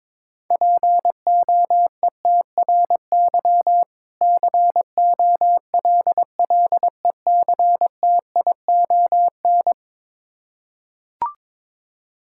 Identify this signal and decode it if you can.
Morse code, heard as a keyed tone, about 22 wpm, 700 Hz. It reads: POETRY COLLECTION
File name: Difficult Words with Hints - 22wpm